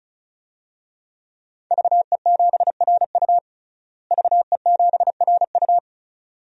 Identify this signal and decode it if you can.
Morse code, heard as a keyed tone, about 34 wpm, 700 Hz. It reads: VE7RU VE7RU
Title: DXCC-100 Call Signs - 5 Characters - Repeated 2x - 35wpm